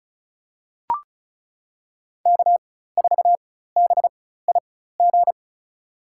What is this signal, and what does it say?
Morse code, heard as a keyed tone, about 34 wpm, spaced as for 16 wpm, 700 Hz. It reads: K4BIG
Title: USA Call Signs - 5 Characters - 4x Character Spacing - 35wpm